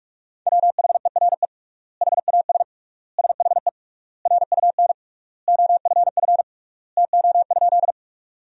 Morse code, heard as a keyed tone, about 46 wpm, 700 Hz. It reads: WHERE HAS SHE RUN OFF TO?
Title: 3x Word-Spacing - Sentences from Top 300 Words - 45wpm